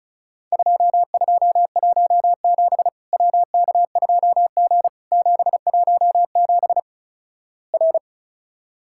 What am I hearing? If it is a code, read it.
Morse code, heard as a keyed tone, about 35 wpm, 700 Hz. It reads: 2217 WK2G 717 R
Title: Worked All Europe DX Contest - QTC Exchange - 35wpm